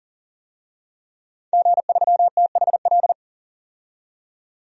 Morse code, heard as a keyed tone, about 40 wpm, 700 Hz. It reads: G3THL
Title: DXCC-100 Call Signs - 5 Characters - 40wpm